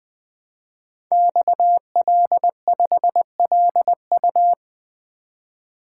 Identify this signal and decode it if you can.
Morse code, heard as a keyed tone, about 20 wpm, 700 Hz. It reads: XL5LU
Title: Code Groups - 5 Letter:Numbers - 20wpm